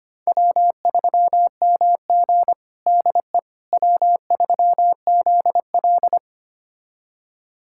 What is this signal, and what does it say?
Morse code, heard as a keyed tone, about 25 wpm, 700 Hz. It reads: W3MG DE W3ZL
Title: USA Callsign DE Callsign - 4 Characters - 25wpm